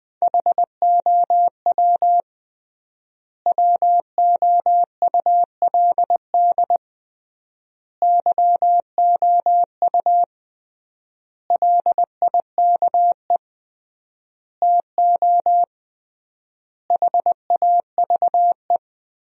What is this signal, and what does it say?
Morse code, heard as a keyed tone, about 20 wpm, 700 Hz. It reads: HOW WOULD YOU LIKE TO HAVE
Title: 3x Word-Spacing - Sentences from Top 100 Words - 20wpm